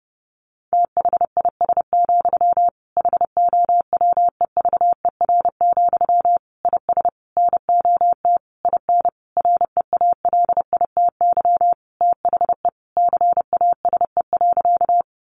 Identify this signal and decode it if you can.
Morse code, heard as a keyed tone, about 28 wpm, 700 Hz. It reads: THIS, HOWEVER, IS NOT IN REALITY THE CASE.
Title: Sentences from Top 1200 Words 30wpm